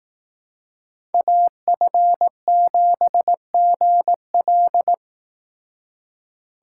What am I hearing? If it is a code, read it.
Morse code, heard as a keyed tone, about 18 wpm, 700 Hz. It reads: AF7GL
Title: Speed Racing - USA 2x2 Call Signs 15wpm